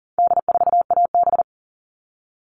Morse code, heard as a keyed tone, about 39 wpm, 700 Hz. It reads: D4AB
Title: All Cut Numbers - 4-Digit 40wpm